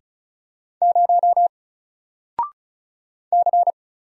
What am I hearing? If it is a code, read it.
Morse code, heard as a keyed tone, about 37 wpm, 700 Hz. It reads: 0 C
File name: Single-Letter-Number-35wpm